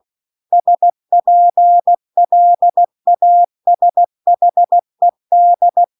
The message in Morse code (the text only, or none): SPLASHED